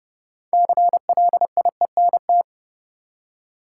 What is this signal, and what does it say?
Morse code, heard as a keyed tone, about 32 wpm, 700 Hz. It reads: CLIENT
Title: Six Letter Words - 30wpm